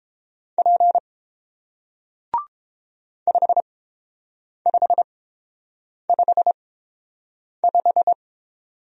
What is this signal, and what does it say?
Morse code, heard as a keyed tone, about 30 wpm, spaced as for 7 wpm, 700 Hz. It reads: P 5555